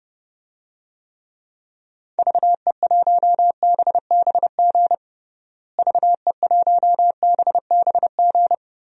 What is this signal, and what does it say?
Morse code, heard as a keyed tone, about 32 wpm, 700 Hz. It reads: VE1BBG VE1BBG
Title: DXCC-100 Call Signs - 6 Characters - Repeated 3x - 30wpm